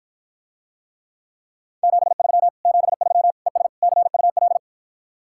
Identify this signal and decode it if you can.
Morse code, heard as a keyed tone, about 52 wpm, 700 Hz. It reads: 73 73 ES CUL